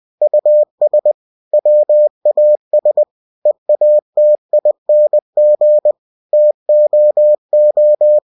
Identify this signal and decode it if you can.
Morse code, heard as a keyed tone, about 20 wpm, 600 Hz. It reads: US WAS EATING TOO
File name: cw-books / winnie-the-pooh / winnie-20wpm-page-0017